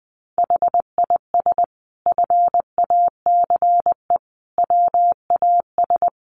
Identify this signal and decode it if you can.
Morse code, heard as a keyed tone, about 19 wpm, 700 Hz. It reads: HIS FACE WAS
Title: Sentences from Top 1200 Words 20wpm